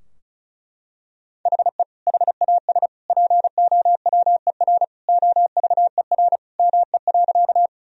Morse code, heard as a keyed tone, about 34 wpm, 700 Hz. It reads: HE HAS POWER OVER ME.
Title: Sentences from Top 1000 Words - Review Part 2 - 35wpm